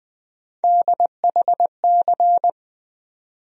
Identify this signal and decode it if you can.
Morse code, heard as a keyed tone, about 20 wpm, 700 Hz. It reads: DHC
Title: USA Call Sign Postfixes - 3 Letters 20wpm